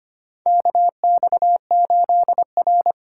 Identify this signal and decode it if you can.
Morse code, heard as a keyed tone, about 25 wpm, 700 Hz. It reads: KX8R
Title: USA Call Signs - 2x Character Spacing - 25wpm